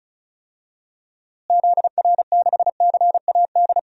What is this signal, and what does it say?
Morse code, heard as a keyed tone, about 36 wpm, 700 Hz. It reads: ZR6CAD
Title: DXCC-100 Call Signs - 6 Characters - Repeated 2x - 35wpm